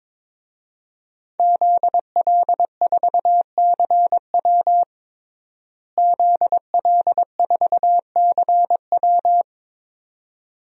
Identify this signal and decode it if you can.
Morse code, heard as a keyed tone, about 22 wpm, 700 Hz. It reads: ZL4CW ZL4CW
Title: DXCC-100 Call Signs - 5 Characters - Repeated 2x - 22wpm